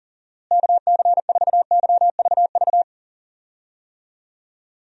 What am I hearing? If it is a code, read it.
Morse code, heard as a keyed tone, about 40 wpm, 700 Hz. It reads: KC4YVV
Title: USA Call Signs - 6 Characters - Repeated 3x - 40wpm